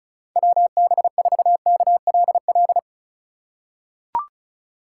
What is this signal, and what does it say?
Morse code, heard as a keyed tone, about 36 wpm, 700 Hz. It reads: WB4KLL
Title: USA Call Signs - 6 Characters - 2x Character Spacing - 35wpm